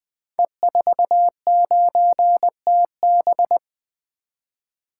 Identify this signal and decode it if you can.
Morse code, heard as a keyed tone, about 20 wpm, 700 Hz. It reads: E49TB
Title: Code Groups - 5 Letter:Numbers - 20wpm